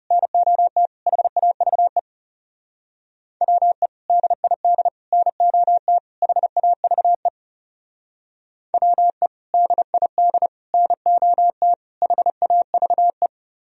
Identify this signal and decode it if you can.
Morse code, heard as a keyed tone, about 39 wpm, 700 Hz. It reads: NOT HAVE WE DID NOT HAVE WE DID NOT HAVE